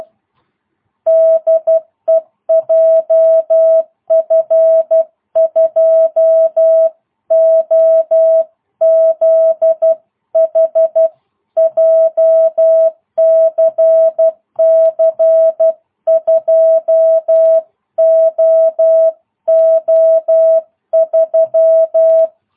Morse code, heard as a keyed tone, about 12 wpm, 660 Hz. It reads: DE JF2OZH JCC 2OO3